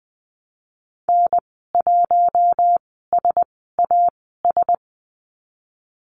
Morse code, heard as a keyed tone, about 20 wpm, spaced as for 15 wpm, 700 Hz. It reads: N1SAS
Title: 088 Lesson 15g - Number 5 - Call Signs 20wpm